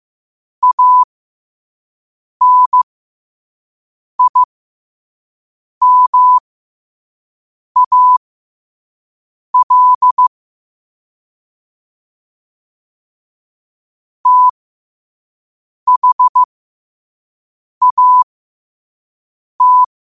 Morse code, heard as a keyed tone, about 15 wpm, spaced as for 5 wpm, 1000 Hz. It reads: ANIMAL THAT
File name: 2024-10-07-5wpm